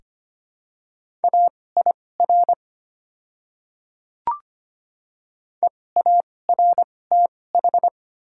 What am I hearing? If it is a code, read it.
Morse code, heard as a keyed tone, about 25 wpm, spaced as for 18 wpm, 700 Hz. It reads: AIR EARTH
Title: 2x Character Spacing - Top 200 Words - 25wpm